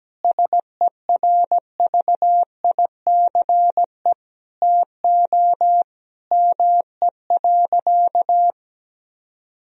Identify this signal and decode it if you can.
Morse code, heard as a keyed tone, about 17 wpm, 700 Hz. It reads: SERVICE TO ME.
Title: Sentences from Top 1000 Words - Review Part 4 - 17wpm